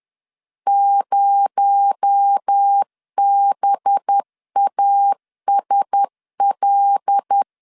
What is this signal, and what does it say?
Morse code, heard as a keyed tone, about 10 wpm, 800 Hz. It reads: 0BASL